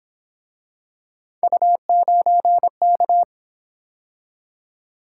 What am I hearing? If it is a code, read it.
Morse code, heard as a keyed tone, about 28 wpm, 700 Hz. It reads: U9K